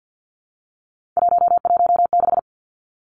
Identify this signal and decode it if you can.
Morse code, heard as a keyed tone, about 50 wpm, 700 Hz. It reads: 116
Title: Three Challenging Letter-Number Combinations 50wpm